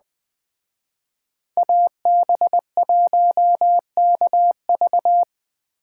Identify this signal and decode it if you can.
Morse code, heard as a keyed tone, about 20 wpm, 700 Hz. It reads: AB1KV